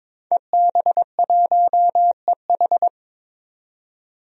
Morse code, heard as a keyed tone, about 21 wpm, 700 Hz. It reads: EB1EH
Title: DXCC-100 Call Signs - 5 Characters - 22wpm